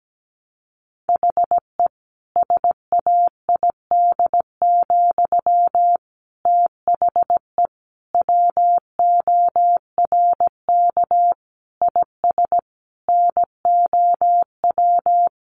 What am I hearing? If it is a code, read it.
Morse code, heard as a keyed tone, about 17 wpm, 700 Hz. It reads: HE SAID, THE WORK IS NOW
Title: Sentences from Top 400 Words 17wpm